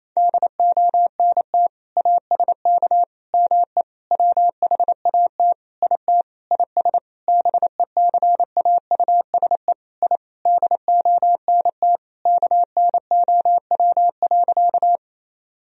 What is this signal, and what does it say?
Morse code, heard as a keyed tone, about 28 wpm, 700 Hz. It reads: DONT ASK ME WHAT IT IS BECAUSE I DONT KNOW.
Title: Sentences from Top 1000 Words - Review Part 3 - 28wpm